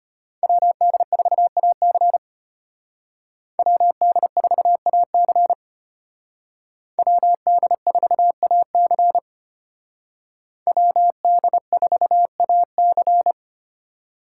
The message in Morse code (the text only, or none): WD4AC WD4AC WD4AC WD4AC